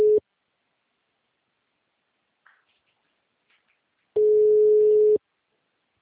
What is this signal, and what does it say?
Ringback tone (Europe, 425 Hz).